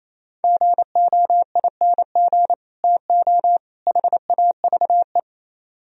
Morse code, heard as a keyed tone, about 34 wpm, 700 Hz. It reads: GOING TO HAVE
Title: Speed-Racing - Sets of 3 Words 28wpm